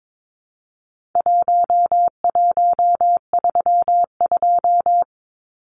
Morse code, heard as a keyed tone, about 22 wpm, 700 Hz. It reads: 1132
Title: Numbers - Contest Style - 4-Digit 22wpm